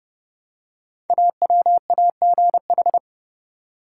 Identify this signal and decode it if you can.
Morse code, heard as a keyed tone, about 28 wpm, 700 Hz. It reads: AWAGH
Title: Code Groups - 5 Letter:Numbers - 30wpm